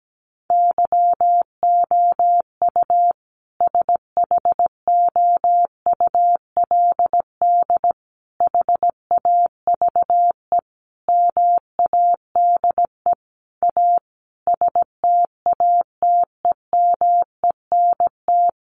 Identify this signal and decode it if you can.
Morse code, heard as a keyed tone, about 17 wpm, 700 Hz. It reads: YOU SHOULD HAVE MADE A STATEMENT
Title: Sentences from Top 1200 Words 17wpm